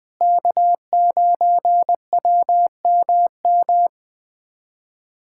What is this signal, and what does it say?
Morse code, heard as a keyed tone, about 21 wpm, 700 Hz. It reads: K9WMM
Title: USA Callsign DE Callsign - 5 Characters - 20wpm